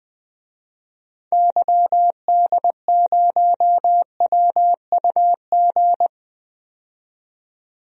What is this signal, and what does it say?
Morse code, heard as a keyed tone, about 20 wpm, 700 Hz. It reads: YD0WUG
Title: DXCC-100 Call Signs - 6 Characters - 20wpm